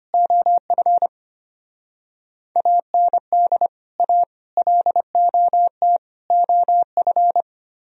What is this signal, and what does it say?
Morse code, heard as a keyed tone, about 24 wpm, 700 Hz. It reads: OF AND A LOT OF
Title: Speed-Racing - Sets of 4 Words 25wpm